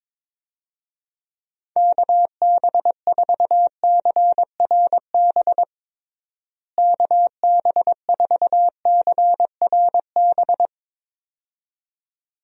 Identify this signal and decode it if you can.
Morse code, heard as a keyed tone, about 22 wpm, 700 Hz. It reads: KB4CRB KB4CRB
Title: USA Call Signs - 6 Characters - Repeated 2x - 22wpm